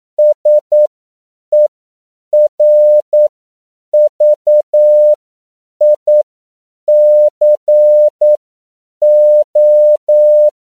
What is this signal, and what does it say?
Morse code, heard as a keyed tone, about 9 wpm, spaced as for 7 wpm, 600 Hz. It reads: SERVICO
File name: CW / 10PPM